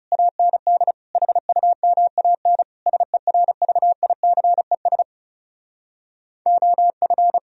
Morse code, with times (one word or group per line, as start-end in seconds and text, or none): 0.1-0.9 AND
1.1-2.6 HUMAN
2.9-5.0 SERVICES
6.5-7.4 OF